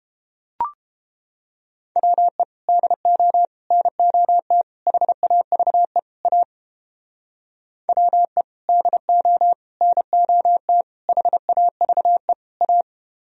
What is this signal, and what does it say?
Morse code, heard as a keyed tone, about 31 wpm, 700 Hz. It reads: WE DO NOT HAVE A WE DO NOT HAVE A